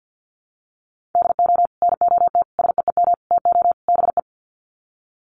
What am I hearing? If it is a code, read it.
Morse code, heard as a keyed tone, about 50 wpm, 700 Hz. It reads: DO NOT SEEM TO BE
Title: Sets of 5 Words Encore2 50wpm